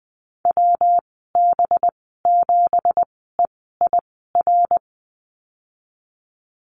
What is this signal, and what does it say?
Morse code, heard as a keyed tone, about 19 wpm, spaced as for 14 wpm, 700 Hz. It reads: WB7EIR
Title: 193 Lesson 31g - Letter v - Call Signs 20wpm